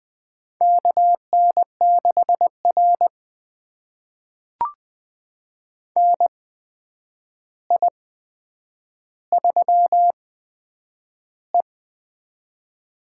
Morse code, heard as a keyed tone, about 20 wpm, spaced as for 9 wpm, 700 Hz. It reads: KN6R NI3E
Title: USA Call Signs - 4 Characters - 8x Character Spacing - 20wpm